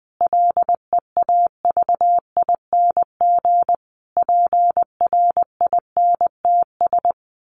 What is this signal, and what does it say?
Morse code, heard as a keyed tone, about 20 wpm, 700 Hz. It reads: LEAVING PRINTS
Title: A Sound Of Thunder 20wpm